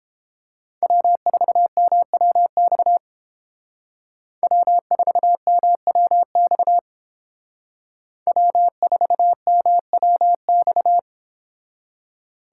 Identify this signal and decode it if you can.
Morse code, heard as a keyed tone, about 29 wpm, 700 Hz. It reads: W4MWX W4MWX W4MWX